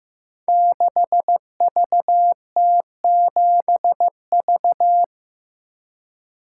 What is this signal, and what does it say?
Morse code, heard as a keyed tone, about 15 wpm, 700 Hz. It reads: 6VT7V
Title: Code Groups - 5 Letter:Numbers - 15wpm